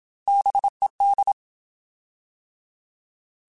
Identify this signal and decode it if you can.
Morse code, heard as a keyed tone, about 27 wpm, 800 Hz. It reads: BED
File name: Alphabetical / Alphabetical mp3s / Bs @25